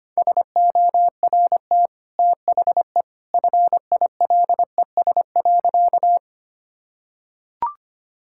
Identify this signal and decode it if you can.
Morse code, heard as a keyed tone, about 25 wpm, 700 Hz. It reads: SORT THE FILES.
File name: Easy Sentences with Hints - 25wpm